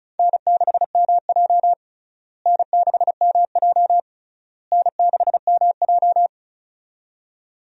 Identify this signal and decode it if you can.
Morse code, heard as a keyed tone, about 36 wpm, 700 Hz. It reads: N6MJ N6MJ N6MJ
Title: USA Call Signs - 4 Characters - Repeated 3x - 35wpm